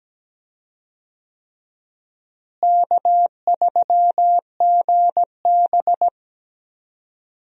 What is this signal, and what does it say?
Morse code, heard as a keyed tone, about 17 wpm, 700 Hz. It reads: K3GB
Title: USA Call Signs - 4 Characters - Repeated 3x - 17wpm